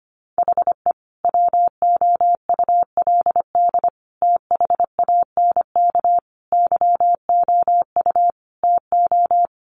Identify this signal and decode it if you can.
Morse code, heard as a keyed tone, about 25 wpm, 700 Hz. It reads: HE WOULD THANK YOU TO
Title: Sentences from Top 1100 Words 25wpm